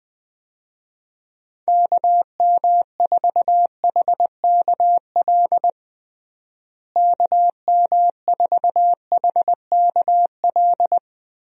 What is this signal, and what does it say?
Morse code, heard as a keyed tone, about 20 wpm, 700 Hz. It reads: KM4HKL KM4HKL
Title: USA Call Signs - 6 Characters - Repeated 2x - 20wpm